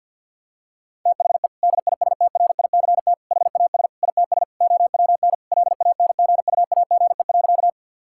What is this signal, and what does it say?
Morse code, heard as a keyed tone, about 50 wpm, 700 Hz. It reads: THE DISTRICT HAS ITS OWN LANGUAGE.
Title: Sentences from Top 1600 Words 50wpm